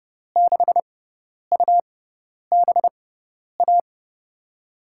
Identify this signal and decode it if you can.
Morse code, heard as a keyed tone, about 30 wpm, spaced as for 10 wpm, 700 Hz. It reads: 6UBA
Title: USA Call Signs - 6 Characters - 6x Character Spacing - 30wpm